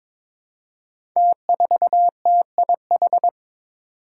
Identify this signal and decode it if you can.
Morse code, heard as a keyed tone, about 22 wpm, 700 Hz. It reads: T4TIH
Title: Five Challenging Letter-Number Combinations 22wpm